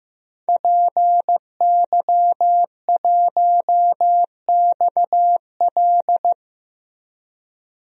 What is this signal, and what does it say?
Morse code, heard as a keyed tone, about 16 wpm, 700 Hz. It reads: PY1XL
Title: DXCC-100 Call Signs - 5 Characters - Repeated 2x - 15wpm